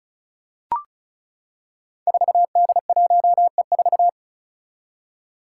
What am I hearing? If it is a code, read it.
Morse code, heard as a keyed tone, about 36 wpm, 700 Hz. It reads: 4D1E4
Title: Five Challenging Letter-Number Combinations 35wpm